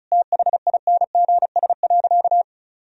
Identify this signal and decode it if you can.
Morse code, heard as a keyed tone, about 34 wpm, 700 Hz. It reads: THINGS.
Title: Sentences from Top 1000 Words - Review Part 4 - 35wpm